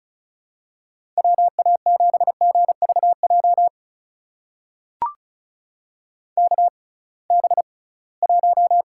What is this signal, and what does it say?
Morse code, heard as a keyed tone, about 36 wpm, spaced as for 23 wpm, 700 Hz. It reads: WA7GVJ KB1
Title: USA Call Signs - 6 Characters - 6x Character Spacing - 35wpm